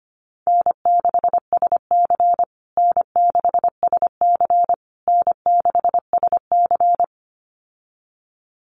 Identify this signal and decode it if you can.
Morse code, heard as a keyed tone, about 25 wpm, 700 Hz. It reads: N6SC N6SC N6SC
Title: USA Call Signs Repeated 3x 25wpm